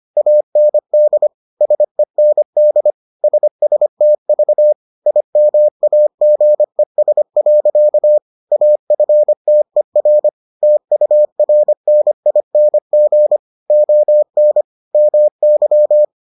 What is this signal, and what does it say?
Morse code, heard as a keyed tone, about 25 wpm, 600 Hz. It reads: AND SEND SSTV IMAGES. AFTER TURNING ON MY